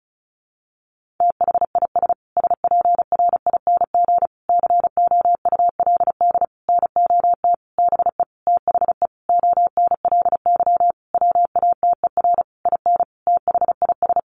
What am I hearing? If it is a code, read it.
Morse code, heard as a keyed tone, about 34 wpm, 700 Hz. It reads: THIS SPRING COULD NOT BE THE ONLY WATER IN THIS
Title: Sentences from Top 1500 Words 35wpm